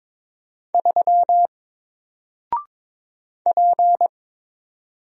Morse code, heard as a keyed tone, about 22 wpm, 700 Hz. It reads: 3 P
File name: Single-Letter-Number-22wpm